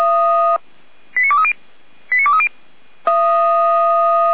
Five-tone selective call (ZVEI), heard as quick runs of five tones, tones of about 70 ms. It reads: 89230 89230
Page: SDR/ZVEI